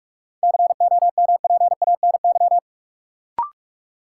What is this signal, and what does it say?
Morse code, heard as a keyed tone, about 46 wpm, 700 Hz. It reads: COMPANY
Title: Sentences from Top 600 Words - New Word Drill - 45wpm